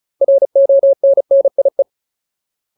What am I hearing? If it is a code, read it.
Morse code, heard as a keyed tone, about 34 wpm, 550 Hz. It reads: RONNIE
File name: Ultimate Word List / 35WPM Ultimate